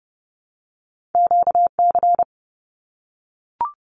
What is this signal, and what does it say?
Morse code, heard as a keyed tone, about 30 wpm, 700 Hz. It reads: QC
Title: Canadian Provinces and Territories 30wpm